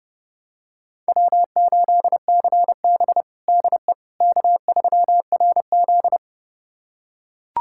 Morse code, heard as a keyed tone, about 30 wpm, 700 Hz. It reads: W8CB DE K3RZ
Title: USA Callsign DE Callsign - 4 Characters - 30wpm